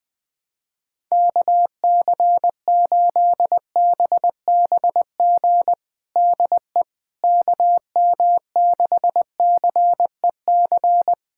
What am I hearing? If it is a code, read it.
Morse code, heard as a keyed tone, about 21 wpm, 700 Hz. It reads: KC8BBG DE KM6CEC
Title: USA Callsign DE Callsign 20wpm